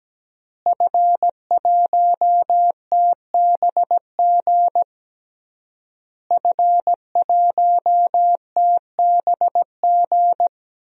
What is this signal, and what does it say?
Morse code, heard as a keyed tone, about 17 wpm, 700 Hz. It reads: F1TBG F1TBG
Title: DXCC-100 Call Signs - 5 Characters - Repeated 3x - 17wpm